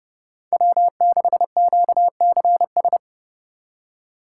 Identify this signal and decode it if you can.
Morse code, heard as a keyed tone, about 30 wpm, 700 Hz. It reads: W6QCS